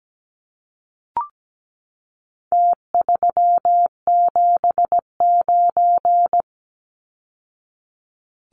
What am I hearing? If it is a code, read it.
Morse code, heard as a keyed tone, about 17 wpm, 700 Hz. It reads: T379
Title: Numbers - Contest Style - 4-Digit 17wpm